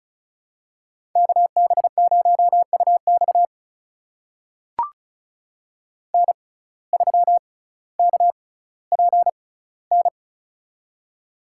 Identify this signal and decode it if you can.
Morse code, heard as a keyed tone, about 36 wpm, spaced as for 18 wpm, 700 Hz. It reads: KB0UX N3KPN
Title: USA Call Signs - 5 Characters - 6x Character Spacing - 35wpm